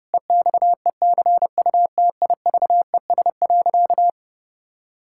Morse code, heard as a keyed tone, about 32 wpm, 700 Hz. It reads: EXECUTIVES.
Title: Pangram Challenge v2 - 30wpm